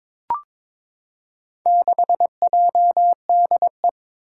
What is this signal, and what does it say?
Morse code, heard as a keyed tone, about 22 wpm, 700 Hz. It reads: 6JDE
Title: Four Challenging Letter-Number Combinations 22wpm